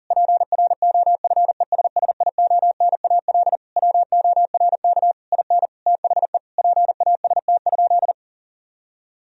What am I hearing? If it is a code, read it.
Morse code, heard as a keyed tone, about 40 wpm, 700 Hz. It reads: PROFESSIONAL WORK IN THE PAST?
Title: Sentences from Top 1000 Words - Review Part 4 - 40wpm